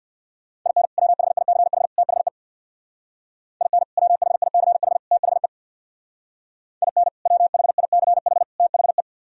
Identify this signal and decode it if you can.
Morse code, heard as a keyed tone, about 57 wpm, 700 Hz. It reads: IN WHICH THE IN WHICH THE IN WHICH THE
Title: Speed-Racing - Sets of 3 Words 50wpm